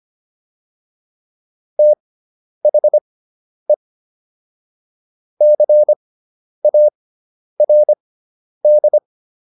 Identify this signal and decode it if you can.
Morse code, heard as a keyed tone, about 25 wpm, spaced as for 10 wpm, 600 Hz. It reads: THE CARD